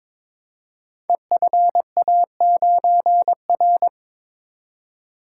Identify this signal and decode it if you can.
Morse code, heard as a keyed tone, about 22 wpm, 700 Hz. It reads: EFA9R
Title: Code Groups - 5 Letter:Numbers - 22wpm